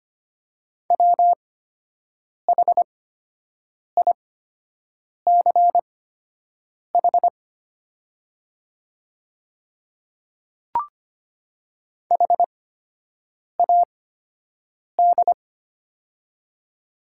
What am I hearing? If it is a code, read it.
Morse code, heard as a keyed tone, about 25 wpm, spaced as for 7 wpm, 700 Hz. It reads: WHICH HAD
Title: 8x Character Spacing - Top 100 Words - 25wpm